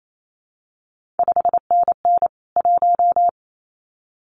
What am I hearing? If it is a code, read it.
Morse code, heard as a keyed tone, about 28 wpm, 700 Hz. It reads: 5NN 1
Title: CQ WW DX Contest Exchange Encore - 28wpm